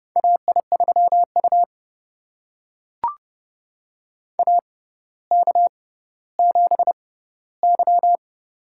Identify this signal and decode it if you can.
Morse code, heard as a keyed tone, about 30 wpm, spaced as for 15 wpm, 700 Hz. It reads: AI3U AK7Y
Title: USA Call Signs - 4 Characters - 6x Character Spacing - 30wpm